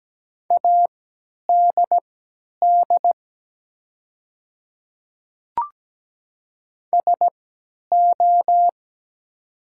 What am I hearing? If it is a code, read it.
Morse code, heard as a keyed tone, about 17 wpm, spaced as for 10 wpm, 700 Hz. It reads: ADD SO
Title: 3x Character Spacing - Top 200 Words - 17wpm